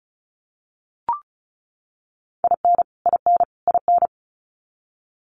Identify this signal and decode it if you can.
Morse code, heard as a keyed tone, about 34 wpm, 700 Hz. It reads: IN IN IN
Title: Top 300 Words Repeated 35wpm